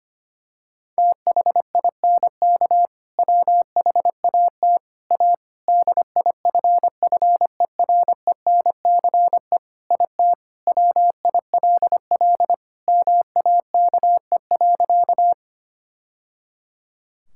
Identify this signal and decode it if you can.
Morse code, heard as a keyed tone, about 25 wpm, 700 Hz. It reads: THINK WHAT A DIFFERENCE IT WILL MAKE.
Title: Sentences from Top 1000 Words - Review Part 5 - 25wpm